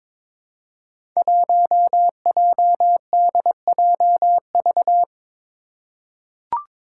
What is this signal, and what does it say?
Morse code, heard as a keyed tone, about 22 wpm, 700 Hz. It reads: 1JDJV
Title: Five Challenging Letter-Number Combinations 22wpm